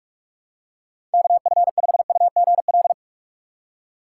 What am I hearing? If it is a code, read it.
Morse code, heard as a keyed tone, about 46 wpm, 700 Hz. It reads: KF5UGL
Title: USA Call Signs - 4x Character Spacing - 45wpm